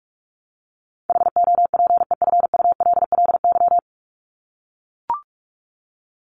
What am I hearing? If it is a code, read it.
Morse code, heard as a keyed tone, about 44 wpm, 700 Hz. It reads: HOPEFULLY
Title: Top 1251-1500 Words 45wpm